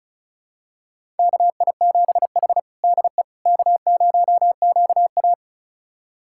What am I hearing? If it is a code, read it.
Morse code, heard as a keyed tone, about 36 wpm, 690 Hz. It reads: KI7H DE K0QA
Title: USA Callsign DE Callsign - 4 Characters - 35wpm